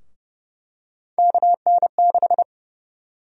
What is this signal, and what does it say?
Morse code, heard as a keyed tone, about 30 wpm, 700 Hz. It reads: KN6
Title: USA Call Sign Prefixes - Repeated 2x - 30wpm